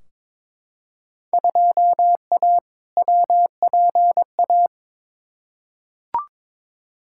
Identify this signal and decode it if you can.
Morse code, heard as a keyed tone, about 22 wpm, 700 Hz. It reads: 2A WPA
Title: Field Day Exchanges - 22wpm